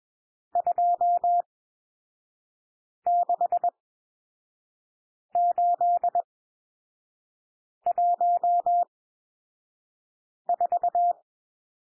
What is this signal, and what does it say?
Morse code, heard as a keyed tone, about 21 wpm, spaced as for 5 wpm, 700 Hz. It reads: 26814
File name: cwmp3 / N1 SOUND NUMBERS 21 wpm